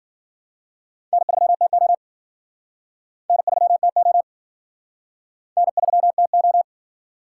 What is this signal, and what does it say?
Morse code, heard as a keyed tone, about 52 wpm, 700 Hz. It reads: N3TO N3TO N3TO